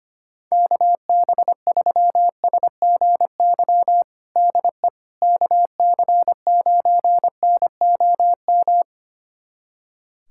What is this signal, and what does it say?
Morse code, heard as a keyed tone, about 25 wpm, 700 Hz. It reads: KB3SGY DE KC9NOM
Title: USA Callsign DE Callsign - 6 Characters - 25wpm